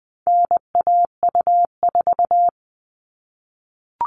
Morse code, heard as a keyed tone, about 20 wpm, 700 Hz. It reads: NAU4
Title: All Cut Numbers - 4-Digit 20wpm